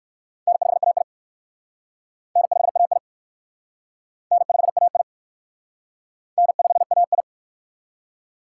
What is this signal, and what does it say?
Morse code, heard as a keyed tone, about 54 wpm, 700 Hz. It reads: N5AI N5AI N5AI N5AI
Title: Speed Racing - USA 1x2 Call Signs 45wpm